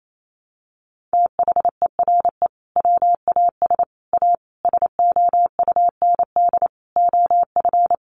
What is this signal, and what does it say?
Morse code, heard as a keyed tone, about 28 wpm, 700 Hz. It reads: THERE WAS A SOUND OF
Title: A Sound Of Thunder 28wpm